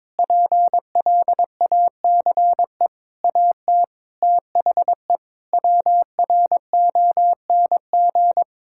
Morse code, heard as a keyed tone, about 22 wpm, 700 Hz. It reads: PLACE AT THE WRONG
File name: Sets of 9 Words - 22wpm